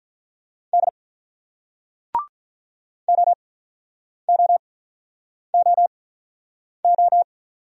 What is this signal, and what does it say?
Morse code, heard as a keyed tone, about 46 wpm, spaced as for 9 wpm, 700 Hz. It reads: D OOOO